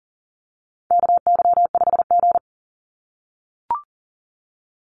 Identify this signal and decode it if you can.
Morse code, heard as a keyed tone, about 40 wpm, 700 Hz. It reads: KY5G
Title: USA Call Signs 40wpm